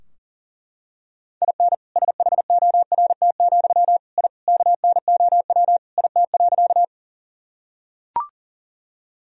Morse code, heard as a keyed tone, about 40 wpm, 700 Hz. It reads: IN SHORT, I KNOW IT.
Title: Sentences from Top 1000 Words - Review Part 2 - 40wpm